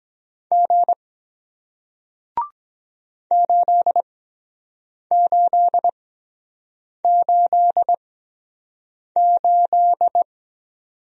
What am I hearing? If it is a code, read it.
Morse code, heard as a keyed tone, about 23 wpm, spaced as for 7 wpm, 700 Hz. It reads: G 8888